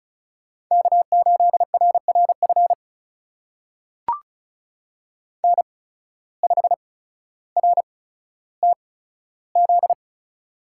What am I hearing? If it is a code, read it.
Morse code, heard as a keyed tone, about 34 wpm, spaced as for 15 wpm, 700 Hz. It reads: K8RRF N5RTZ
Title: USA Call Signs - 5 Characters - 8x Character Spacing - 35wpm